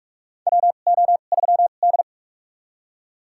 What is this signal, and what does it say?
Morse code, heard as a keyed tone, about 43 wpm, spaced as for 32 wpm, 700 Hz. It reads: WO3D
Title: USA Call Signs - 4 Characters - 2x Character Spacing - 45wpm